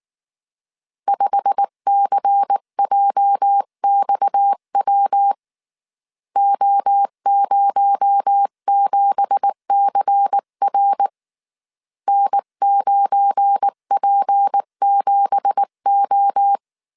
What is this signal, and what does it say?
Morse code, heard as a keyed tone, about 19 wpm, 800 Hz. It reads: 5CJXW O07CR N9P7O
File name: CW / DET / 1 9 1500C S C 18wpm 2539